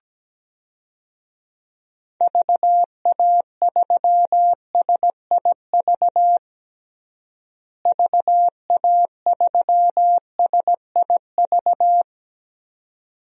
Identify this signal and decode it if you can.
Morse code, heard as a keyed tone, about 17 wpm, 700 Hz. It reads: VA3SIV VA3SIV
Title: DXCC-100 Call Signs - 6 Characters - Repeated 2x - 17wpm